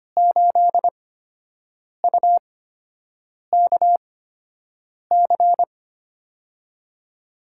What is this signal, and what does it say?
Morse code, heard as a keyed tone, about 25 wpm, spaced as for 7 wpm, 700 Hz. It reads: 8UKC
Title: USA Call Signs - 5 Characters - 8x Character Spacing - 25wpm